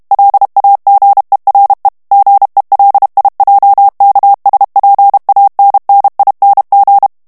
Morse code, heard as a keyed tone, about 32 wpm, 800 Hz. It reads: LAGERE GELIJKSPANNING
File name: HOLLANDAIS texte clair 2 vitesse 30 mots minute